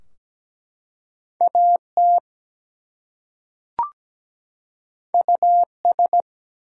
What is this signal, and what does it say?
Morse code, heard as a keyed tone, about 17 wpm, 700 Hz. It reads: AT US
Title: Two Letter Words - 17wpm